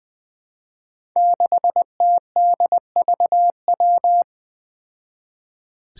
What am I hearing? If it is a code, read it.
Morse code, heard as a keyed tone, about 20 wpm, 700 Hz. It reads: 6TDVW
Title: Five Challenging Letter-Number Combinations 20wpm